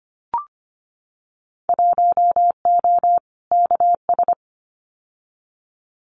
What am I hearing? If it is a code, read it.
Morse code, heard as a keyed tone, about 25 wpm, 700 Hz. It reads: 1O KS
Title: Winter Field Day Exchanges 25wpm